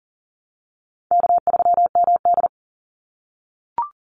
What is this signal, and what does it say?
Morse code, heard as a keyed tone, about 40 wpm, 700 Hz. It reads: K3MD
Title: Easy Call Signs - 40wpm